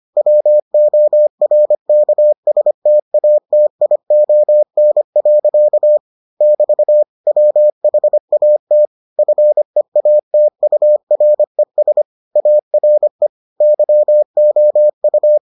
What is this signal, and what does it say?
Morse code, heard as a keyed tone, about 25 wpm, 600 Hz. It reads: WORKSTATION. <BT> WHAT FEATURES ARE YOU